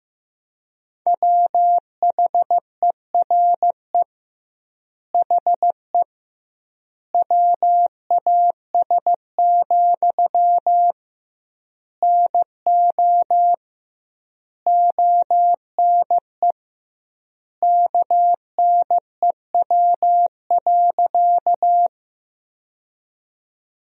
Morse code, heard as a keyed tone, about 14 wpm, 700 Hz. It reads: WHERE HE WAS, NO ONE KNEW.
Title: 2x Word-Spacing - Sentences from Top 400 Words - 15wpm